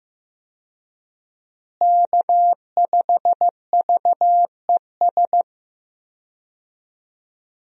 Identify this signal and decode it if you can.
Morse code, heard as a keyed tone, about 15 wpm, 700 Hz. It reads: K5VES
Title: USA Call Signs - 5 Characters - Repeated 3x - 15wpm